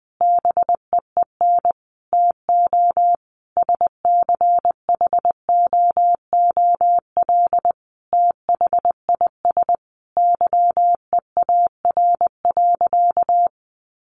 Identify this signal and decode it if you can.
Morse code, heard as a keyed tone, about 20 wpm, 700 Hz. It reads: BEEN TO SCHOOL THIS YEAR.
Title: Sentences from Top 1400 Words 20wpm